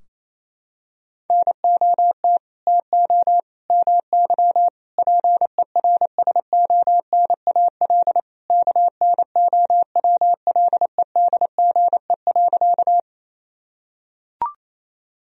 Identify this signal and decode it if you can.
Morse code, heard as a keyed tone, about 28 wpm, 700 Hz. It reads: NOT TO MY PERSONAL KNOWLEDGE.
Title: Sentences from Top 1000 Words - Review Part 4 - 28wpm